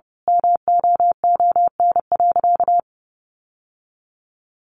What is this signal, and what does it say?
Morse code, heard as a keyed tone, about 30 wpm, 700 Hz. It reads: MOON.
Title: Sentences from Top 1300 Words 30wpm